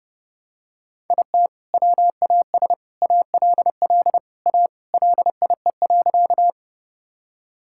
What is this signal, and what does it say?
Morse code, heard as a keyed tone, about 32 wpm, 700 Hz. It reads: IT WAS ALL A LIE.
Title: Sentences from Top 1600 Words 30wpm